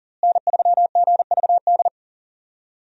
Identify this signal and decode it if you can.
Morse code, heard as a keyed tone, about 40 wpm, 700 Hz. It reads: N3GVD
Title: USA Call Signs - 5 Characters - Repeated 2x - 40wpm